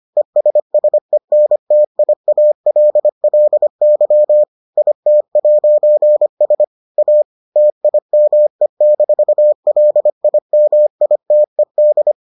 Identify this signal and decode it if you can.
Morse code, heard as a keyed tone, about 25 wpm, 600 Hz. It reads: ESSENTIALLY IT'S A TIME-LIMITED